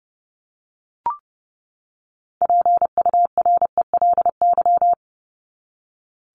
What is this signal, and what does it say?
Morse code, heard as a keyed tone, about 28 wpm, 700 Hz. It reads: PURELY
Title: LY Suffix - Top 100 Words with Base Words 30wpm